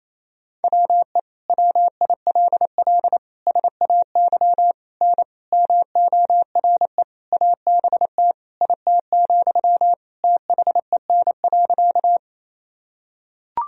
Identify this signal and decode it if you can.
Morse code, heard as a keyed tone, about 28 wpm, 700 Hz. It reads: WE WILL SAY N MORE ABT IT, THEN.
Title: Sentences from Top 200 Words and Abbreviations 28wpm